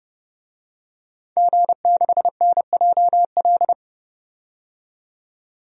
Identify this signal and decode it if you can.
Morse code, heard as a keyed tone, about 30 wpm, 700 Hz. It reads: G6NJL
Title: DXCC-100 Call Signs - 5 Characters - 30wpm